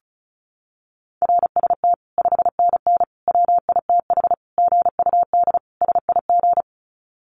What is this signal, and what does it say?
Morse code, heard as a keyed tone, about 34 wpm, 700 Hz. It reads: RST 5NN WITH GUD SIG
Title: Common QSO Phrases 35wpm